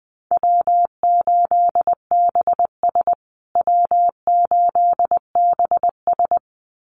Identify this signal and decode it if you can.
Morse code, heard as a keyed tone, about 20 wpm, 700 Hz. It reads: W8BS W8BS
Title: USA Call Signs Repeated 2x 20wpm